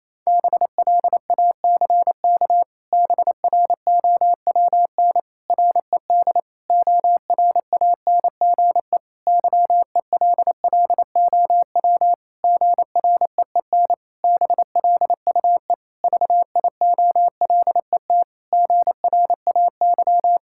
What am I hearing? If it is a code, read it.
Morse code, heard as a keyed tone, about 28 wpm, 700 Hz. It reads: BLACK BROWN RED ORANGE YELLOW GREEN BLUE VIOLET GRAY
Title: Pangram Challenge v2 - 28wpm